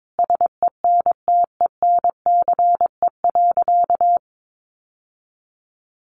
Morse code, heard as a keyed tone, about 22 wpm, 700 Hz. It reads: SENTENCE.
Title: Sentences from Top 1200 Words 22wpm